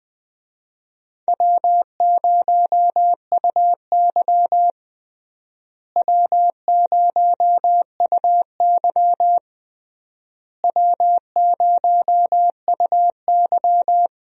Morse code, heard as a keyed tone, about 21 wpm, 700 Hz. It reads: W0UY W0UY W0UY
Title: USA Call Signs - 4 Characters - Repeated 3x - 20wpm